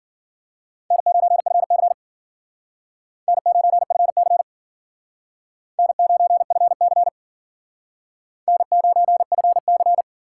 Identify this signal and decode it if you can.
Morse code, heard as a keyed tone, about 52 wpm, 700 Hz. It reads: N9FC N9FC N9FC N9FC